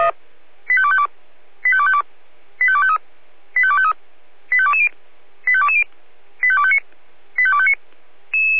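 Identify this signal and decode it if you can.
Five-tone selective call (ZVEI), heard as quick runs of five tones, tones of about 70 ms. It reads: 85252 85252 85253 85253 85220 85220 85259 85259
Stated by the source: ZVEI five-tone selective call received by SDR.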